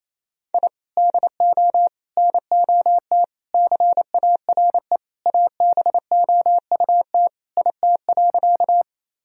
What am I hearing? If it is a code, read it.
Morse code, heard as a keyed tone, about 28 wpm, 700 Hz. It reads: I DO NOT CARE ABOUT IT.
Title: Sentences from Top 1000 Words - Review Part 1 - 28wpm